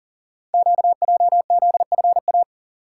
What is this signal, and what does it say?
Morse code, heard as a keyed tone, about 39 wpm, 700 Hz. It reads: QJZFA
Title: Code Groups - 5 Letters - 40wpm